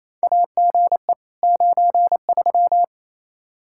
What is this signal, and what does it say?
Morse code, heard as a keyed tone, about 27 wpm, 700 Hz. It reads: AGE 93
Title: Practice QSO - Exchange 3 - More Info - 28wpm